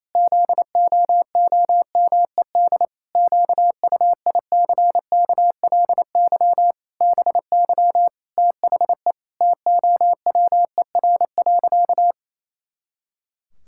Morse code, heard as a keyed tone, about 28 wpm, 700 Hz. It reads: ZOOMED QUICKLY BY THE TOWER.
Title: Pangram Challenge v2 - 28wpm